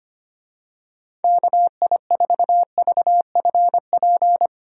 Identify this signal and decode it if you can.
Morse code, heard as a keyed tone, about 25 wpm, 700 Hz. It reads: KI4VFP
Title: USA Call Signs - 6 Characters - 25wpm